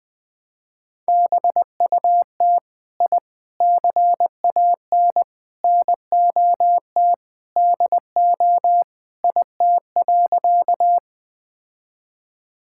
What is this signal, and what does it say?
Morse code, heard as a keyed tone, about 20 wpm, 700 Hz. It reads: BUT I CAN NOT DO IT.
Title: Sentences with 4-Word Phrases - 20wpm